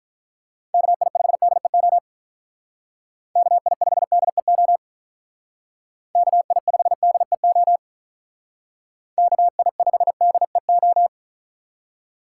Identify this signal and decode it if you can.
Morse code, heard as a keyed tone, about 43 wpm, 700 Hz. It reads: KI5DEO KI5DEO KI5DEO KI5DEO
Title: Speed Racing - USA 2x3 Call Signs 35wpm